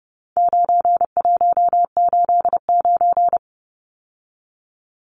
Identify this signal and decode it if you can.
Morse code, heard as a keyed tone, about 30 wpm, 700 Hz. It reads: 9189
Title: Numbers - 4-Digit 30wpm